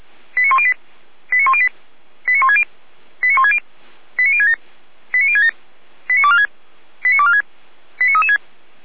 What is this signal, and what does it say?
Five-tone selective call (ZVEI), heard as quick runs of five tones, tones of about 70 ms. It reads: 89108 89108 89160 89160 89067 89067 89246 89246 89336